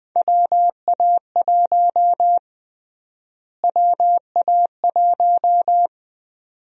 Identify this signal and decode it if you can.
Morse code, heard as a keyed tone, about 20 wpm, 700 Hz. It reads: WA1 WA1
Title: USA Call Sign Prefixes - Repeated 2x - 20wpm